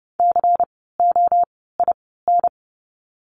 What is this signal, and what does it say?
Morse code, heard as a keyed tone, about 30 wpm, 700 Hz. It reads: C O I N
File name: Lesson 16f - Letter c - Words 30wpm